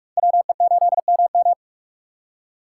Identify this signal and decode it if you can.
Morse code, heard as a keyed tone, about 57 wpm, 700 Hz. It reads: WE8MM